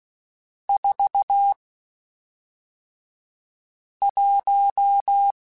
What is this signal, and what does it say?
Morse code, heard as a keyed tone, about 16 wpm, spaced as for 3 wpm, 800 Hz. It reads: 41
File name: Spaced Characters / Spaced Characters-mp3s / Weather4 14 sp @5